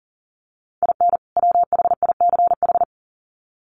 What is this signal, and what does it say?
Morse code, heard as a keyed tone, about 40 wpm, 700 Hz. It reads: IN WHICH
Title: Sets of 2 Words 40wpm